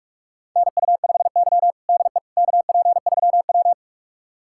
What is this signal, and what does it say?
Morse code, heard as a keyed tone, about 46 wpm, 700 Hz. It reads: NU5Y DE KP3W
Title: USA Callsign DE Callsign - 4 Characters - 45wpm